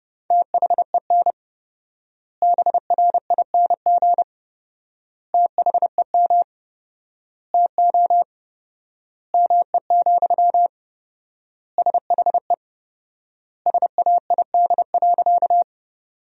Morse code, heard as a keyed tone, about 30 wpm, 700 Hz. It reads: THEN BRING THEM TO ME, SHE SAID.
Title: 4x Word-Spacing - Sentences from Top 500 Words - 30wpm